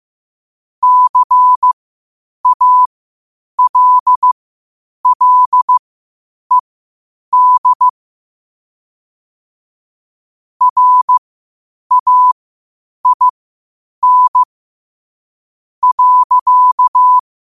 Morse code, heard as a keyed tone, about 15 wpm, spaced as for 8 wpm, 1000 Hz. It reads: CALLED RAIN.
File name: qotdfiles / 8wpm